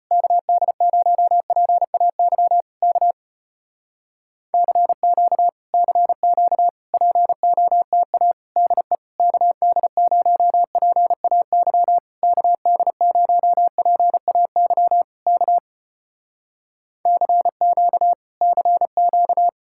Morse code, heard as a keyed tone, about 35 wpm, 700 Hz. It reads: KD0PAY K CQ CQ POTA DE KD0PAY KD0PAY K CQ CQ